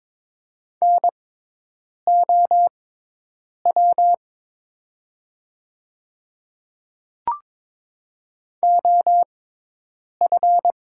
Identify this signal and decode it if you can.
Morse code, heard as a keyed tone, about 22 wpm, spaced as for 8 wpm, 690 Hz. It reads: NOW OF
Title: 6x Character Spacing - Top 300 Words - 22wpm